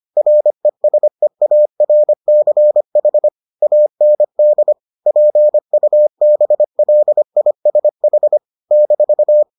Morse code, heard as a keyed tone, about 25 wpm, 600 Hz. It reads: RESEARCH AND PUBLISH -